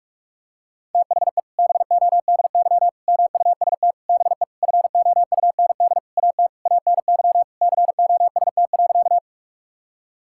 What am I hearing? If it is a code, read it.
Morse code, heard as a keyed tone, about 41 wpm, 700 Hz. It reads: THE BODY MUST BE FOUND AT ANY COST.
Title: Sentences from Top 1000 Words - Review Part 5 - 45wpm